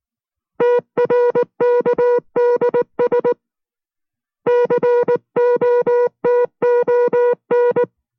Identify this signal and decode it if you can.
Morse code, heard as a keyed tone, about 19 wpm, 480 Hz. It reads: TRKDS COTON